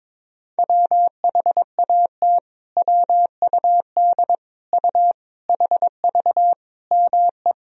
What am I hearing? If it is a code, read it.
Morse code, heard as a keyed tone, about 21 wpm, 700 Hz. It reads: WHAT WUD U HV ME
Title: Sentences from Top 200 Words and Abbreviations 22wpm